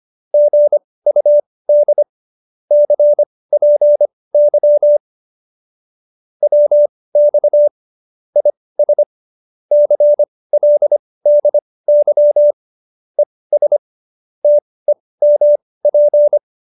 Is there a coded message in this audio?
Yes: Morse code from 0.3 to 16.4 s, heard as a keyed tone, about 25 wpm, spaced as for 18 wpm, 590 Hz. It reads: GUD CPY WX IS CLDY ES TEMP